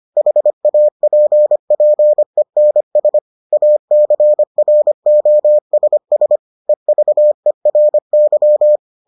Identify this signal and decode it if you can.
Morse code, heard as a keyed tone, about 25 wpm, 600 Hz. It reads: HAPPENS ACROSS EVERY